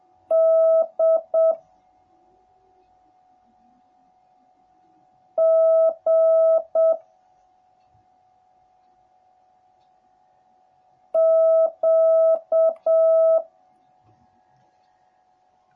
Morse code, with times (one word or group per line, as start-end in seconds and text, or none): 0.3-1.5 D
5.4-6.9 G
11.1-13.4 Q